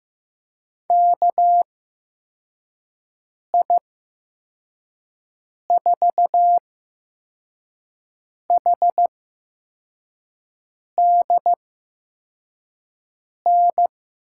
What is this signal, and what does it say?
Morse code, heard as a keyed tone, about 15 wpm, spaced as for 4 wpm, 700 Hz. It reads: KI4HDN
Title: USA Call Signs - 6 Characters - 8x Character Spacing - 15wpm